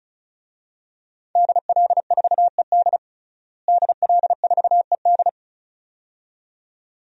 Morse code, heard as a keyed tone, about 36 wpm, 700 Hz. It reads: DL4ED DL4ED
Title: DXCC-100 Call Signs - 5 Characters - Repeated 2x - 35wpm